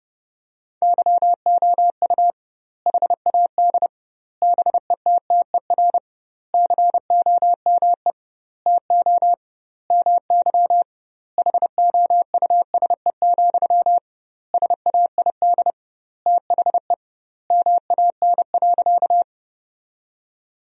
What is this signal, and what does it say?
Morse code, heard as a keyed tone, about 30 wpm, 700 Hz. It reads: YOU HAD BETTER COME TO MY HOUSE, SAID THE MAN.
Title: 2x Word-Spacing - Sentences from Top 400 Words - 30wpm